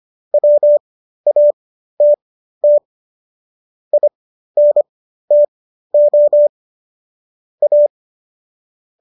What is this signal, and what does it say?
Morse code, heard as a keyed tone, about 25 wpm, spaced as for 13 wpm, 590 Hz. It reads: WATT INTO A